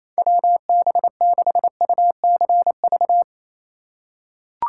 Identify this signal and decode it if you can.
Morse code, heard as a keyed tone, about 28 wpm, 700 Hz. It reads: WB6UCV
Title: USA Call Signs - 6 Characters - 2x Character Spacing - 28wpm